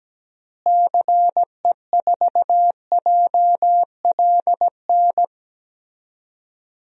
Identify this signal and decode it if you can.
Morse code, heard as a keyed tone, about 17 wpm, 700 Hz. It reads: CE4JLN
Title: DXCC-100 Call Signs - 6 Characters - Repeated 3x - 17wpm